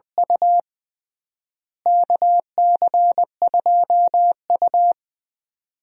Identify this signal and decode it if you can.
Morse code, heard as a keyed tone, about 20 wpm, 700 Hz. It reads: U KC2U
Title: USA Call Signs - 4 Characters - Repeated 2x - 20wpm